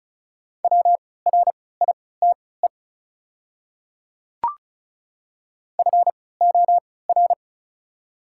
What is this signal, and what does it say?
Morse code, heard as a keyed tone, about 36 wpm, spaced as for 20 wpm, 700 Hz. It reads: WRITE FOR
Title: 3x Character Spacing - Top 300 Words - 35wpm